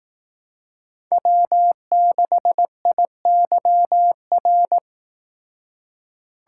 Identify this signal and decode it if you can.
Morse code, heard as a keyed tone, about 18 wpm, 700 Hz. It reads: W6IYR
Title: Speed Racing - USA 1x3 Call Signs 15wpm